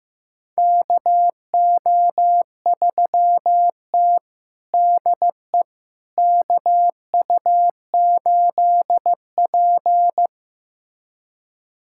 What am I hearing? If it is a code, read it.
Morse code, heard as a keyed tone, about 15 wpm, 700 Hz. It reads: KO3T DE KU8P
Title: USA Callsign DE Callsign - 4 Characters - 15wpm